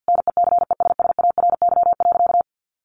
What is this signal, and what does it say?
Morse code, heard as a keyed tone, about 50 wpm, 700 Hz. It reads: NECESSARY.
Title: Sentences from Top 800 Words 50wpm